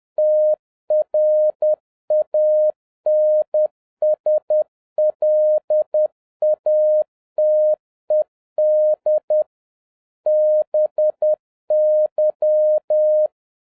Morse code, heard as a keyed tone, about 10 wpm, 610 Hz. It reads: TRANSLATED BY
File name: 2011-07-mp3 / n21 GREETINGS FROM SINGAPORE 10wpm